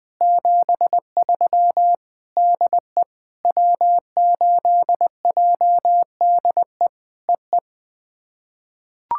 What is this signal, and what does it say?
Morse code, heard as a keyed tone, about 21 wpm, 700 Hz. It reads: 73 DE W8JDE EE
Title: POTA QSO Practice - 20wpm